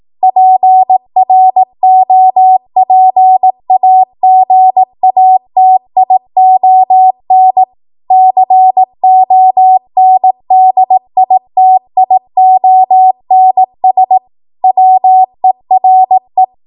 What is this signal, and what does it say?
Morse code, heard as a keyed tone, about 18 wpm, 750 Hz. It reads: PROPAGATION CONDITIONS WERE
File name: Morse / 18WPM